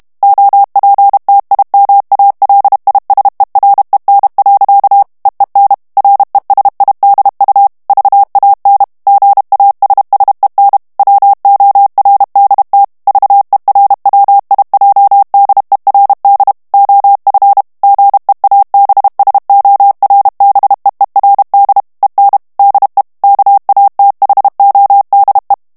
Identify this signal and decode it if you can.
Morse code, heard as a keyed tone, about 32 wpm, 800 Hz. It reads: OPTIMALISEREN. EEN RESIDU VAN GASSEN WORDT VERWIJDERD OF GEABSORBEERD EN DE KATHODE
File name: morse mp3 / HOLLANDAIS texte clair 2 vitesse 30 mots minute